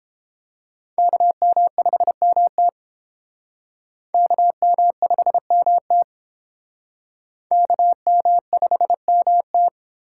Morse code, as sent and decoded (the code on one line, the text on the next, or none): -.- -- ..... -- - / -.- -- ..... -- - / -.- -- ..... -- -
KM5MT KM5MT KM5MT